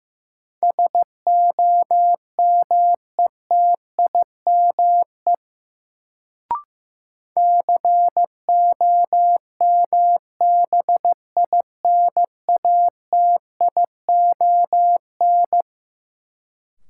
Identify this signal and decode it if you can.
Morse code, heard as a keyed tone, about 15 wpm, 700 Hz. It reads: SOMETIME COMBINATION
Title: Sentences from Top 1000 Words - Review Part 6 - New Word Drill - 15wpm